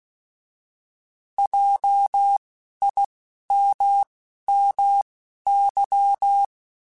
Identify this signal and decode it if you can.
Morse code, heard as a keyed tone, about 16 wpm, spaced as for 11 wpm, 800 Hz. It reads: JIMMY
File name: Alphabetical / Alphabetical mp3s / Js @10